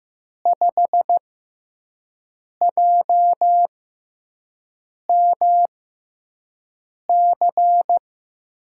Morse code, heard as a keyed tone, about 15 wpm, spaced as for 5 wpm, 700 Hz. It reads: 5JMC
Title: USA Call Signs - 5 Characters - 6x Character Spacing - 15wpm